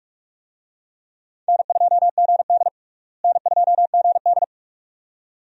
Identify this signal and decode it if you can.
Morse code, heard as a keyed tone, about 46 wpm, 700 Hz. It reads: N2GD N2GD
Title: USA Call Signs - 4 Characters - Repeated 2x - 45wpm